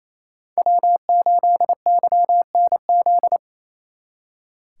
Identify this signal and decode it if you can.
Morse code, heard as a keyed tone, about 37 wpm, 700 Hz. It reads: W8YNZ